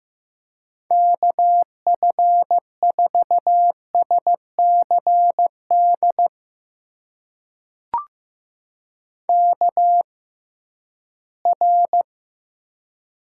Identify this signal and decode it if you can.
Morse code, heard as a keyed tone, about 15 wpm, spaced as for 11 wpm, 700 Hz. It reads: KF4SCD KR
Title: USA Call Signs - 6x Character Spacing - 15wpm